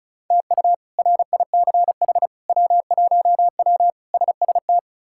Morse code, heard as a keyed tone, about 36 wpm, 700 Hz. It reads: TU RICH W1W SST
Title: K1USN SST QSO Practice - 35wpm